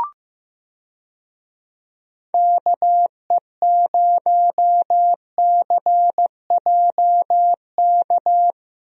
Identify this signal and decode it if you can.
Morse code, heard as a keyed tone, about 15 wpm, 700 Hz. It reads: KE0CJK
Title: USA Call Signs - 6 Characters - Repeated 2x - 15wpm